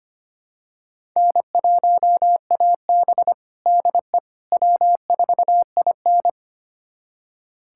Morse code, heard as a keyed tone, about 25 wpm, 700 Hz. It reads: N1AB DE W4IN
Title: USA Callsign DE Callsign - 4 Characters - 25wpm